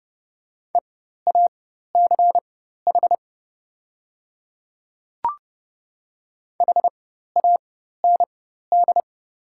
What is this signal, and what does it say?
Morse code, heard as a keyed tone, about 30 wpm, spaced as for 14 wpm, 700 Hz. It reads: EACH HAND
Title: 4x Character Spacing - Top 200 Words - 30wpm